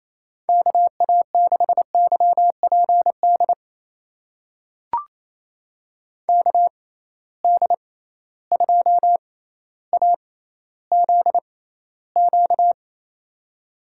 Morse code, heard as a keyed tone, about 28 wpm, spaced as for 14 wpm, 700 Hz. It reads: KA6YPD KD2AZQ
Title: USA Call Signs - 6 Characters - 6x Character Spacing - 28wpm